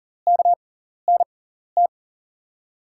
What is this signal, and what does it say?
Morse code, heard as a keyed tone, about 40 wpm, spaced as for 14 wpm, 700 Hz. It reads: KNT
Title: USA Call Signs - 6 Characters - 6x Character Spacing - 40wpm